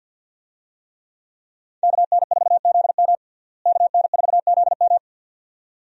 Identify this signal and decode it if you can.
Morse code, heard as a keyed tone, about 52 wpm, 700 Hz. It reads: KN4ZM KN4ZM
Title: USA Call Signs - 5 Characters - Repeated 2x - 50wpm